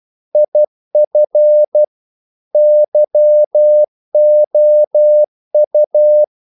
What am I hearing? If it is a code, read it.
Morse code, heard as a keyed tone, about 12 wpm, 600 Hz. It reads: IF YOU
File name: CW / Practice / If12